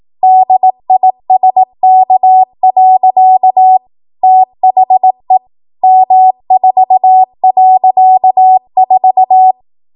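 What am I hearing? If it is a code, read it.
Morse code, heard as a keyed tone, about 18 wpm, 750 Hz. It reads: DISK. THE M4.4
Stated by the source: 18 WPM Morse code practice from a national amateur radio association's code archive.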